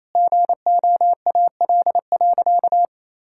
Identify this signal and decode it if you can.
Morse code, heard as a keyed tone, about 28 wpm, 700 Hz. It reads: GOAL.
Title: Sentences from Top 1000 Words - Review Part 5 - 28wpm